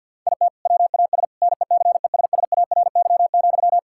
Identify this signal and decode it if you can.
Morse code, heard as a keyed tone, about 50 wpm, 700 Hz. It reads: IT WAS NECESSARY,
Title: Sentences from Top 1600 Words 50wpm